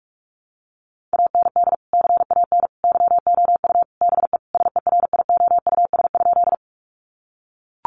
Morse code, heard as a keyed tone, about 44 wpm, 700 Hz. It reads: AND CAN YOU BE SERIOUS?
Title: Sentences from Top 1100 Words 45wpm